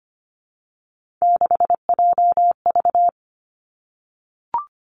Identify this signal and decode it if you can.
Morse code, heard as a keyed tone, about 25 wpm, 700 Hz. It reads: 6JV
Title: Three Challenging Letter-Number Combinations 25wpm